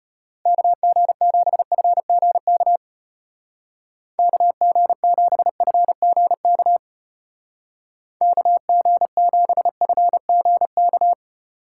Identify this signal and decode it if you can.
Morse code, heard as a keyed tone, about 32 wpm, 700 Hz. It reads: KG7FGK KG7FGK KG7FGK